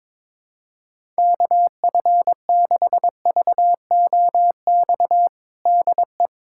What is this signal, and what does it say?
Morse code, heard as a keyed tone, about 22 wpm, 700 Hz. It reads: KF6VOX DE
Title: USA Callsign DE Callsign - 6 Characters - 22wpm